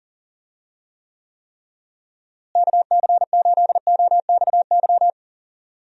Morse code, heard as a keyed tone, about 40 wpm, 700 Hz. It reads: KC8OXY
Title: USA Call Signs - 6 Characters - Repeated 3x - 40wpm